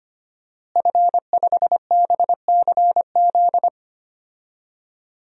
Morse code, heard as a keyed tone, about 25 wpm, 700 Hz. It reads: F5BCZ
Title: DXCC-100 Call Signs - 5 Characters - Repeated 3x - 25wpm